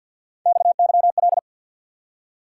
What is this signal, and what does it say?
Morse code, heard as a keyed tone, about 53 wpm, 700 Hz. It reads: XYL
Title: Top 90 QSO Elements 50wpm